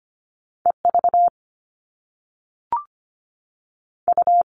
Morse code, heard as a keyed tone, about 25 wpm, 700 Hz. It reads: EV U
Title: All Cut Numbers - 2-Digit 25wpm